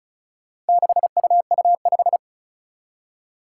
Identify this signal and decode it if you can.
Morse code, heard as a keyed tone, about 36 wpm, 700 Hz. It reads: 6UU5
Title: Four Challenging Letter-Number Combinations 35wpm